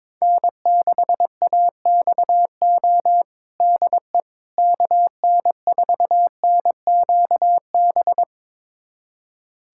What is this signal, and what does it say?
Morse code, heard as a keyed tone, about 22 wpm, 700 Hz. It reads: N6AXO DE KN4NQB
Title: USA Callsign DE Callsign 22wpm